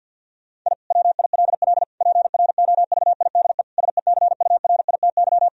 Morse code, heard as a keyed tone, about 50 wpm, 700 Hz. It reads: I WILL PROVIDE SECURITY
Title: Sentences from Top 1000 Words - Review Part 4 - 50wpm